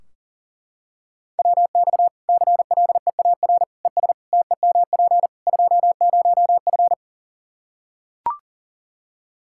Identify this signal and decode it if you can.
Morse code, heard as a keyed tone, about 40 wpm, 700 Hz. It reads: WX CLEAR ES TEMP 20F
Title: Practice QSO - Exchange 2 - Info - 40wpm